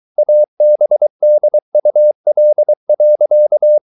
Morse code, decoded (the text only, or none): ABDUL.